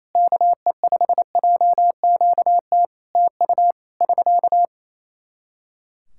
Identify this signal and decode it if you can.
Morse code, heard as a keyed tone, about 28 wpm, 700 Hz. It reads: KE5JQT TU <SK>
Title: Practice QSO - Exchange 4 - Ending - 28wpm